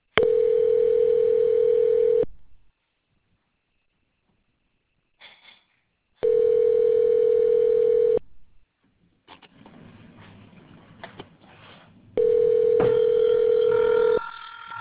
Ringback tone (North America), 440 + 480 Hz.